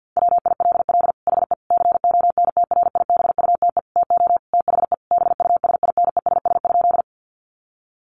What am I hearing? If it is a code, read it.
Morse code, heard as a keyed tone, about 50 wpm, 700 Hz. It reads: WILL HE CONTRIBUTE TO THE BUSINESS?
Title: Sentences from Top 1400 Words 50wpm